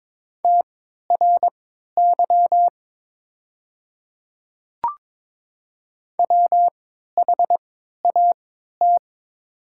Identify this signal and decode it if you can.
Morse code, heard as a keyed tone, about 22 wpm, spaced as for 13 wpm, 700 Hz. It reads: TRY WHAT
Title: 3x Character Spacing - Top 200 Words - 22wpm